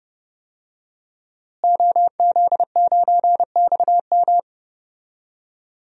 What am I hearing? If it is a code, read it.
Morse code, heard as a keyed tone, about 30 wpm, 700 Hz. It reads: OZ9XM
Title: DXCC-100 Call Signs - 5 Characters - Repeated 2x - 30wpm